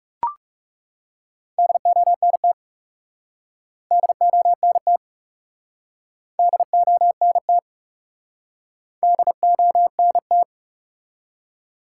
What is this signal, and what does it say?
Morse code, heard as a keyed tone, about 40 wpm, 700 Hz. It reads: DONT DONT DONT DONT